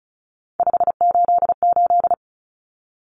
Morse code, heard as a keyed tone, about 34 wpm, 700 Hz. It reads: 588
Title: Numbers - 3-Digit 35wpm